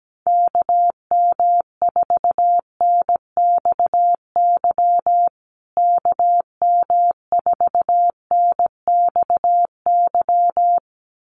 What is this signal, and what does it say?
Morse code, heard as a keyed tone, about 17 wpm, 700 Hz. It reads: KM4NXY KM4NXY
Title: USA Call Signs Repeated 2x 17wpm